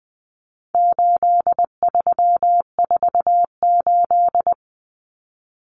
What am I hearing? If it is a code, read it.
Morse code, heard as a keyed tone, about 20 wpm, 700 Hz. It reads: 8348
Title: Common Cut Numbers - 4-Digit 20wpm